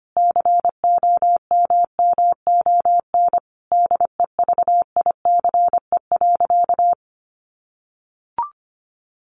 Morse code, heard as a keyed tone, about 25 wpm, 700 Hz. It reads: COMMON DEVICE.
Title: Sentences from Top 1500 Words 25wpm